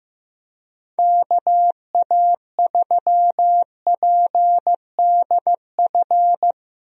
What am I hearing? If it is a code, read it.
Morse code, heard as a keyed tone, about 15 wpm, 700 Hz. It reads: KA3PDF
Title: USA Call Signs - 6 Characters - Repeated 3x - 15wpm